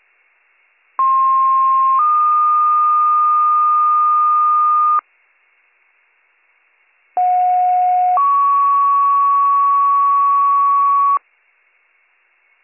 Two-tone page, heard as a one-second tone, then a three-second tone, about 1060 then 1220 Hz, about 730 then 1120 Hz. It is heard from 1.0 to 11.2 s.